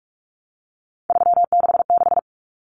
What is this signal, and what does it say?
Morse code, heard as a keyed tone, about 44 wpm, 700 Hz. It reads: 366
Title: Numbers - 3-Digit 45wpm